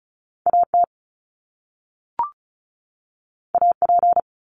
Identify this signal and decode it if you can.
Morse code, heard as a keyed tone, about 36 wpm, 700 Hz. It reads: AT AP
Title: Top Two Letter Combinations in Words 35wpm